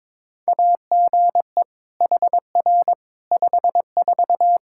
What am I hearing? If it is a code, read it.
Morse code, heard as a keyed tone, about 22 wpm, 700 Hz. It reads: AGE HR 54
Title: Practice QSO - Exchange 3 - More Info - 22wpm